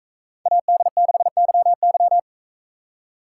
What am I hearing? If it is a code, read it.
Morse code, heard as a keyed tone, about 44 wpm, 700 Hz. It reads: AD6YY